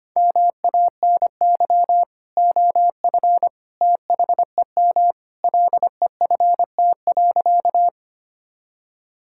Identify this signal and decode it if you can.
Morse code, heard as a keyed tone, about 25 wpm, 700 Hz. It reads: MANY OF THEM LEFT.
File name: practice / 25wpm / Sentences with 2-Word Phrases - 25wpm